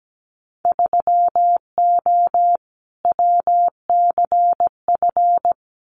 Morse code, heard as a keyed tone, about 17 wpm, 700 Hz. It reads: 3O WCF
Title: Winter Field Day Exchanges 17wpm